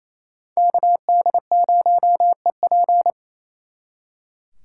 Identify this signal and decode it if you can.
Morse code, heard as a keyed tone, about 36 wpm, 700 Hz. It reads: KD0EP